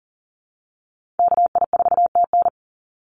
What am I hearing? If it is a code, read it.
Morse code, heard as a keyed tone, about 40 wpm, 700 Hz. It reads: KI4TN
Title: NA SOTA Chaser Callsigns 40wpm